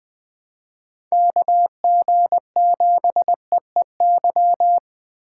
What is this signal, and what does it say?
Morse code, heard as a keyed tone, about 20 wpm, 700 Hz. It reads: KG7EEY
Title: USA Call Signs - 3x Character Spacing - 20wpm